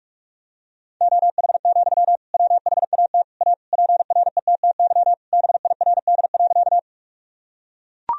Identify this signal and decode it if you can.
Morse code, heard as a keyed tone, about 41 wpm, 700 Hz. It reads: OH, WHAT A PRETTY BIRD.
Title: Sentences from Top 1000 Words - Review Part 2 - 45wpm